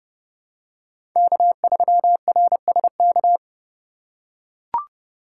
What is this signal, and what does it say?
Morse code, heard as a keyed tone, about 32 wpm, 700 Hz. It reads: K3RSK
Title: USA Call Signs - 5 Characters - 4x Character Spacing - 30wpm